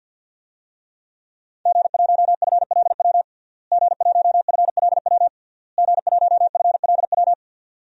Morse code, heard as a keyed tone, about 53 wpm, 700 Hz. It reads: G1FLW G1FLW G1FLW
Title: DXCC-100 Call Signs - 5 Characters - Repeated 3x - 50wpm